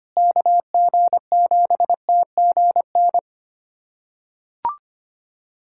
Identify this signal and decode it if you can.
Morse code, heard as a keyed tone, about 25 wpm, 700 Hz. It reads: KG7TGN
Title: USA Callsign DE Callsign 25wpm